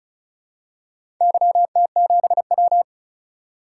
Morse code, heard as a keyed tone, about 36 wpm, 700 Hz. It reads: YT7W
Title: DXCC-100 Call Signs - 4 Characters - 35wpm